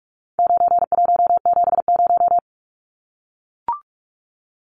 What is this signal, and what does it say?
Morse code, heard as a keyed tone, about 44 wpm, 700 Hz. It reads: 9170
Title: Numbers - 4-Digit 45wpm